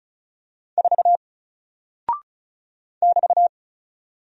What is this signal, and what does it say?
Morse code, heard as a keyed tone, about 36 wpm, 700 Hz. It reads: 4 <BT>
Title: Single Character (Letter:Number:Punctuation:Prosign) Alt 35wpm